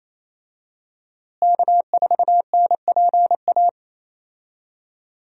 Morse code, heard as a keyed tone, about 28 wpm, 700 Hz. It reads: K4NPA
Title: USA Call Signs - 5 Characters - Repeated 3x - 28wpm